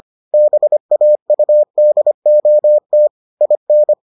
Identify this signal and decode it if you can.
Morse code, heard as a keyed tone, about 25 wpm, 600 Hz. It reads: BAUDOT IN